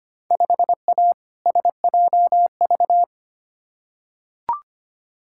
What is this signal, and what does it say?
Morse code, heard as a keyed tone, about 25 wpm, 700 Hz. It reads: 5A SJV
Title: Field Day Exchanges - 25wpm